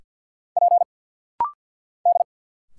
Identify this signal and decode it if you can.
Morse code, heard as a keyed tone, about 50 wpm, 700 Hz. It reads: P D
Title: Single Letters - Rapid-Fire 50wpm